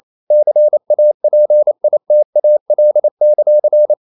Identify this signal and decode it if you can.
Morse code, heard as a keyed tone, about 28 wpm, 600 Hz. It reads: CAPITAL;
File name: CW / Practice / NationalCapitals28